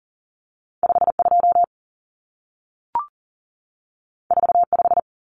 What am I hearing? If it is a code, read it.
Morse code, heard as a keyed tone, about 40 wpm, 700 Hz. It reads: 52 45
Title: Numbers - 2-Digit 40wpm